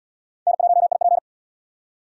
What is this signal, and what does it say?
Morse code, heard as a keyed tone, about 77 wpm, 700 Hz. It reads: N1IP